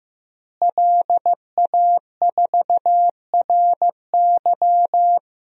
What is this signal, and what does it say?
Morse code, heard as a keyed tone, about 15 wpm, 700 Hz. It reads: LA4RY
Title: DXCC-100 Call Signs - 5 Characters - Repeated 3x - 15wpm